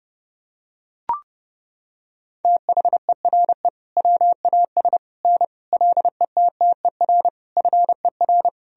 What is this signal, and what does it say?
Morse code, heard as a keyed tone, about 32 wpm, 700 Hz. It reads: THERE WAS N LETTER FER
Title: Sentences from Top 300 Words and Abbreviations 30wpm